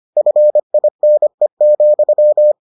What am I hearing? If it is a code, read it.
Morse code, heard as a keyed tone, about 25 wpm, 600 Hz. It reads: FINE,